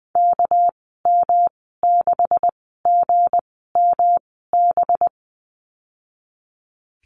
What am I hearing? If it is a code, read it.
Morse code, heard as a keyed tone, about 20 wpm, spaced as for 15 wpm, 700 Hz. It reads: KM6GMB
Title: 219 Lesson 35g - Number 0 - Call Signs 20wpm